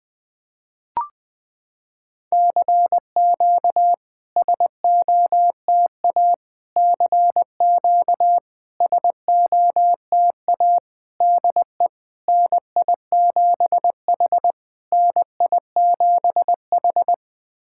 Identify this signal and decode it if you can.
Morse code, heard as a keyed tone, about 20 wpm, 700 Hz. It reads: CQ SOTA CQ SOTA DE NI7H NI7H
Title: SOTA QSO Practice 20wpm